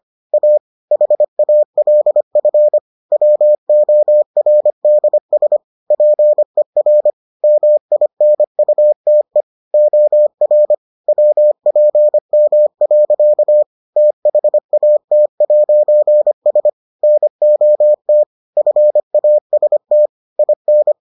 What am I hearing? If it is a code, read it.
Morse code, heard as a keyed tone, about 25 wpm, 600 Hz. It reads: A HALF WORDS PER MINUTE OR WPM. THAT'S NOT FAST IN